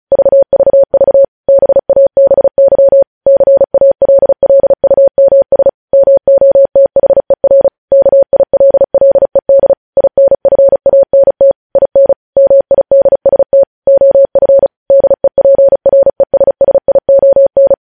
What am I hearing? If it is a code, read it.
Morse code, heard as a keyed tone, about 36 wpm, 550 Hz. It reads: VVV BABY CALLUMS MOTHER KILLED INFANT IN MIDST OF DEPRESSION
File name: podcast / News headlines-1x-35x35